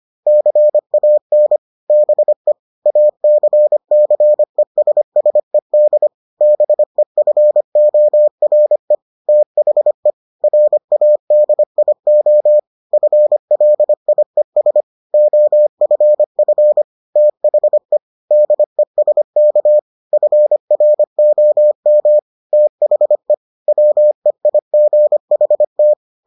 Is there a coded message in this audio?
Yes: Morse code from 0.3 to 25.9 s, heard as a keyed tone, about 25 wpm, 600 Hz. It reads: CAN BE ACCESSED BEFORE THE RADIO FLIES OFF THE DESK FROM THE WEIGHT